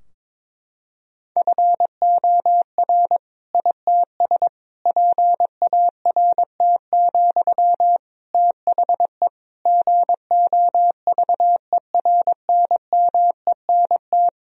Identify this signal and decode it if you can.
Morse code, heard as a keyed tone, about 22 wpm, 700 Hz. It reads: FOR ITS PART, THE GOVERNMENT
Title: Sentences from Top 1000 Words - Review Part 3 - 22wpm